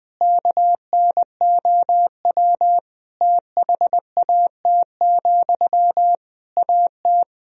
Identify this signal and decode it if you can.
Morse code, heard as a keyed tone, about 19 wpm, 700 Hz. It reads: KNOW THAT, AT
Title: Sentences from Top 1000 Words - Review Part 4 - 20wpm